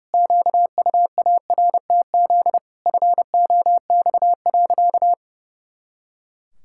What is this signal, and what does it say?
Morse code, heard as a keyed tone, about 32 wpm, 700 Hz. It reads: QUARTZ FOX.
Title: Pangram Challenge v2 - 30wpm